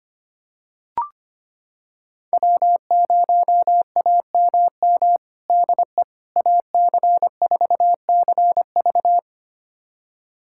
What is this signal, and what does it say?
Morse code, heard as a keyed tone, about 25 wpm, 700 Hz. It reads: W0AMM DE AC4CV
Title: USA Callsign DE Callsign 25wpm